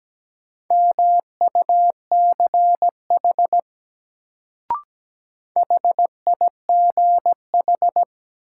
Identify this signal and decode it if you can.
Morse code, heard as a keyed tone, about 17 wpm, 700 Hz. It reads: MUCH HIGH
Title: Sentences from Top 200 Words - New Word Drill - 17wpm